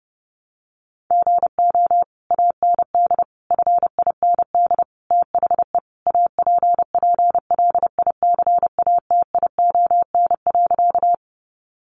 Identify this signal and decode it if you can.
Morse code, heard as a keyed tone, about 30 wpm, 700 Hz. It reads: GO AND FIND THE APPLICATION.
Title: Sentences from Top 1200 Words 30wpm